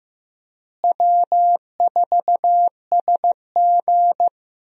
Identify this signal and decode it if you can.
Morse code, heard as a keyed tone, about 15 wpm, 700 Hz. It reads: W4SG
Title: USA Call Signs - 4 Characters - Repeated 2x - 15wpm